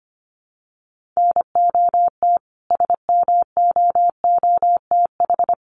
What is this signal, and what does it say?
Morse code, heard as a keyed tone, about 25 wpm, 700 Hz. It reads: NOT SMOOTH
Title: Sentences from Top 1400 Words 25wpm